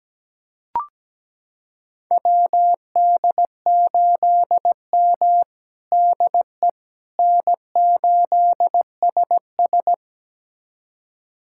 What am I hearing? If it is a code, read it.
Morse code, heard as a keyed tone, about 17 wpm, 700 Hz. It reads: WD8M DE N8SS
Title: USA Callsign DE Callsign - 4 Characters - 17wpm